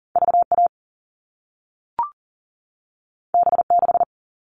Morse code, heard as a keyed tone, about 40 wpm, 700 Hz. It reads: VA B6
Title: All Cut Numbers - 2-Digit 40wpm